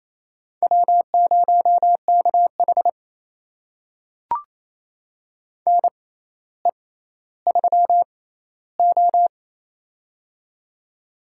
Morse code, heard as a keyed tone, about 28 wpm, spaced as for 14 wpm, 700 Hz. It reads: W0KH NE3O
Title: USA Call Signs - 6x Character Spacing - 28wpm